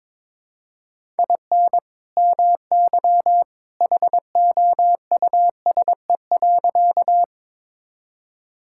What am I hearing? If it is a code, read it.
Morse code, heard as a keyed tone, about 21 wpm, 700 Hz. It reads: IN MY HOUSE.
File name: practice / 22wpm / Sentences with 2-Word Phrases - 22wpm